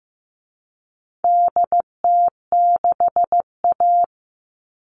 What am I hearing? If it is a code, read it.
Morse code, heard as a keyed tone, about 15 wpm, 700 Hz. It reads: DT6A
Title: All Cut Numbers - 4-Digit 15wpm